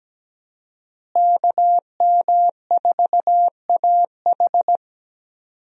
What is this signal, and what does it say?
Morse code, heard as a keyed tone, about 17 wpm, 700 Hz. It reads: KM4AH
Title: USA Call Signs - 5 Characters - Repeated 2x - 17wpm